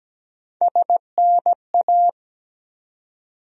Morse code, heard as a keyed tone, about 17 wpm, 700 Hz. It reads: SNA
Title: USA Call Sign Postfixes - 3 Letters 17wpm